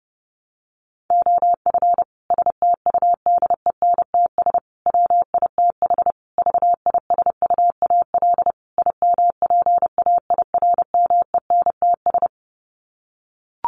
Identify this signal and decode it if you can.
Morse code, heard as a keyed tone, about 30 wpm, 700 Hz. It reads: OF STUDENTS WITH VISUAL IMPAIRMENTS
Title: Sets of 5 Words Encore 30wpm